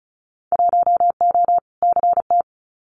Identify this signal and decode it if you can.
Morse code, heard as a keyed tone, about 34 wpm, 700 Hz. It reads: 1O CT
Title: Winter Field Day Exchanges Encore 35wpm